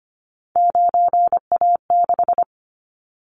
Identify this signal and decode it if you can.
Morse code, heard as a keyed tone, about 25 wpm, 700 Hz. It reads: 9A6
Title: Callsign Prefixes for Top 100 DXCC - 25wpm